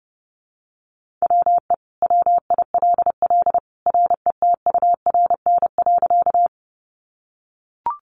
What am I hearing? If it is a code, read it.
Morse code, heard as a keyed tone, about 30 wpm, 700 Hz. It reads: WE WILL RETURN.
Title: Sentences from Top 800 Words 30wpm